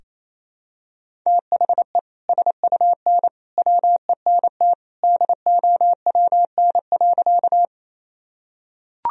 Morse code, heard as a keyed tone, about 28 wpm, 700 Hz. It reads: THE SUN WENT DOWN.
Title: Sentences from Top 1000 Words - Review Part 1 - 28wpm